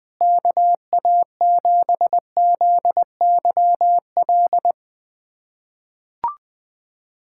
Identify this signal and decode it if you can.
Morse code, heard as a keyed tone, about 20 wpm, 700 Hz. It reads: KA7ZYL
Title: USA Callsign DE Callsign - 6 Characters - 20wpm